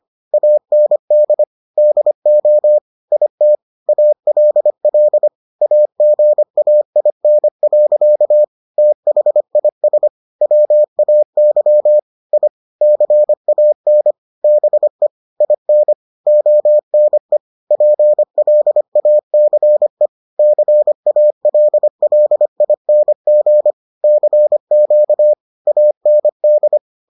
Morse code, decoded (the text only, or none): AND DO IT ALL AGAIN. THIS WAY I CAN BE IN ONE PLACE CALLING CQ AND